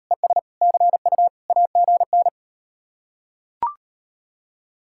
Morse code, heard as a keyed tone, about 39 wpm, 700 Hz. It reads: ES CU AGN